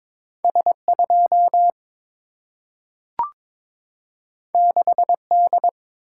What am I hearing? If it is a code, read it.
Morse code, heard as a keyed tone, about 22 wpm, 700 Hz. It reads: S 2 6 D
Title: Single-Letter Single-Number 22wpm